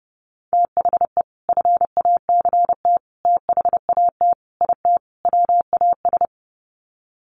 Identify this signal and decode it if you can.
Morse code, heard as a keyed tone, about 30 wpm, 700 Hz. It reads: THE FACT THAT IT WAS
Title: Sets of 5 Words Encore2 30wpm